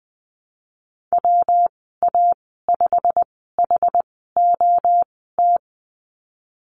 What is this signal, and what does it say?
Morse code, heard as a keyed tone, about 19 wpm, spaced as for 14 wpm, 700 Hz. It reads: WA5HOT
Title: 186 Lesson 30g - Letter b - Call Signs 20wpm